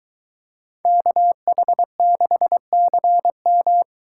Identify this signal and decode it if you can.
Morse code, heard as a keyed tone, about 22 wpm, 700 Hz. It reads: KH6CM